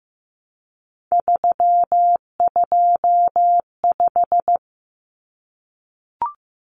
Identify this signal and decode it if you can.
Morse code, heard as a keyed tone, about 15 wpm, 700 Hz. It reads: 325
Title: Numbers - 3-Digit 15wpm